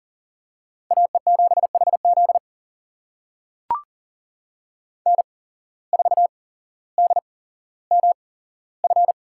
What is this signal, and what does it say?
Morse code, heard as a keyed tone, about 40 wpm, spaced as for 17 wpm, 700 Hz. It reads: AE7HZ N4DMF
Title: USA Call Signs - 5 Characters - 8x Character Spacing - 40wpm